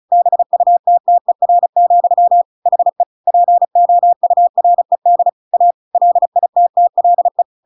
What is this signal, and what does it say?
Morse code, heard as a keyed tone, about 35 wpm, 700 Hz. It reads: BUTTER, HE POURED A LITTLE